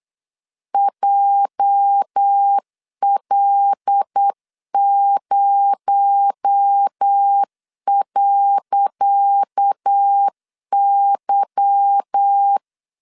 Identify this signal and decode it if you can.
Morse code, heard as a keyed tone, about 8 wpm, 800 Hz. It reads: JL0.Y
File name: LEZIONI / 10 SIGNS / 10C DOT 8wpm 3649